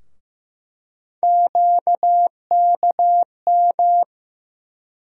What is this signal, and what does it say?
Morse code, heard as a keyed tone, about 15 wpm, 700 Hz. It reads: QKM
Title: USA Call Sign Postfixes - 3 Letters 15wpm